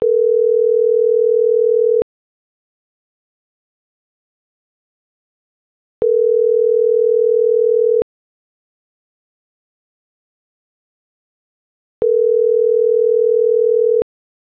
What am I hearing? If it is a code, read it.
Ringback tone (North America), 440 + 480 Hz.